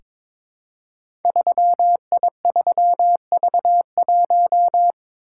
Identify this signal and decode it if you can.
Morse code, heard as a keyed tone, about 22 wpm, 700 Hz. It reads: 3I3V1
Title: Code Groups - 5 Letter:Numbers - 22wpm